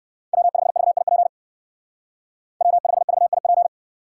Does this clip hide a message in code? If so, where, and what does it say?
Yes: Morse code from 0.3 to 3.7 s, heard as a keyed tone, about 59 wpm, 700 Hz. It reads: W5VIP W5VIP